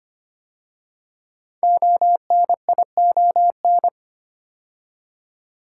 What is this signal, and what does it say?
Morse code, heard as a keyed tone, about 25 wpm, 700 Hz. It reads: ONION
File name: 5 Letter Words - First 3 Letters Given - 25wpm